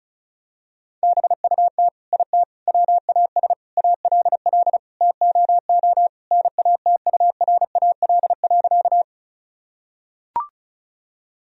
Morse code, heard as a keyed tone, about 36 wpm, 700 Hz. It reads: BUT IT WAS ALL TOO NATURAL.
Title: Sentences from Top 1000 Words - Review Part 4 - 35wpm